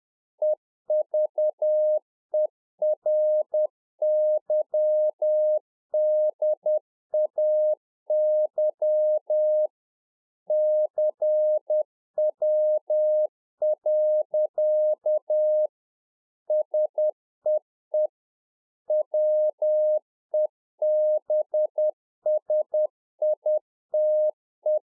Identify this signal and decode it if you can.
Morse code, heard as a keyed tone, about 10 wpm, 610 Hz. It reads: EVERYDAY CW. SEE WEBSITE
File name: 2011-04-mp3 / Czech Army Morse Keys 10WPM